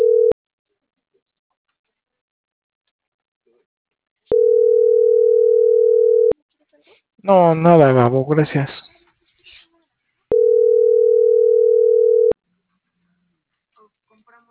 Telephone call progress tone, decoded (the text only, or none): ringback tone